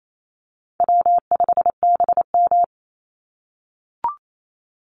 Morse code, heard as a keyed tone, about 28 wpm, 700 Hz. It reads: W5BM
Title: Easy Call Signs - 28wpm